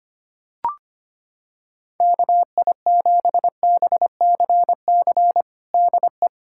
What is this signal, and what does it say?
Morse code, heard as a keyed tone, about 25 wpm, 700 Hz. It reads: KI7BCC DE
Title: USA Callsign DE Callsign 25wpm